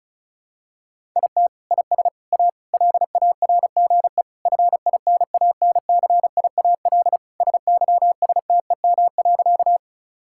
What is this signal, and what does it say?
Morse code, heard as a keyed tone, about 36 wpm, 700 Hz. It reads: IT IS A LARGE FINANCIAL SYSTEM.
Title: Sentences from Top 1000 Words - Review Part 5 - 35wpm